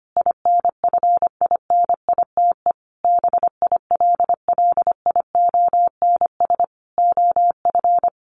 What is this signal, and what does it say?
Morse code, heard as a keyed tone, about 25 wpm, 700 Hz. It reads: INFINITE BILLIONS OF
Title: A Sound Of Thunder 25wpm